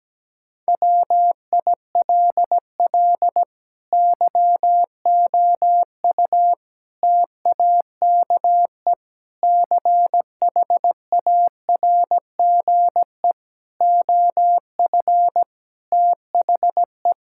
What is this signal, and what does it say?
Morse code, heard as a keyed tone, about 17 wpm, 700 Hz. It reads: WILL YOU TAKE CHARGE OF THE
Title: Sentences from Top 1000 Words - Review Part 6 - 17wpm